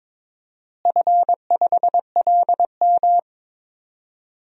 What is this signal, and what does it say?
Morse code, heard as a keyed tone, about 22 wpm, 700 Hz. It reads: F5LM
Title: DXCC-100 Call Signs - 4 Characters - Repeated 3x - 22wpm